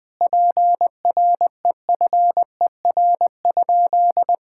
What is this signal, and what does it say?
Morse code, heard as a keyed tone, about 20 wpm, 700 Hz. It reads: PREFER?
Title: Sentences from Top 1600 Words 20wpm